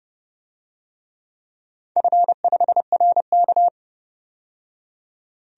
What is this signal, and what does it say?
Morse code, heard as a keyed tone, about 29 wpm, 700 Hz. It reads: F5RK
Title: DXCC-100 Call Signs - 4 Characters - 30wpm